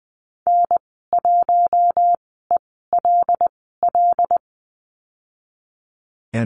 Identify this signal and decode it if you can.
Morse code, heard as a keyed tone, about 20 wpm, spaced as for 15 wpm, 700 Hz. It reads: N1ELL
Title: 128 Lesson 21g - Number 6 - Call Signs 20wpm